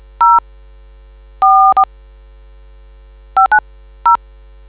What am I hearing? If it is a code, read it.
DTMF, heard as telephone touch tones, tones of about 120 ms. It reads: 0 4 590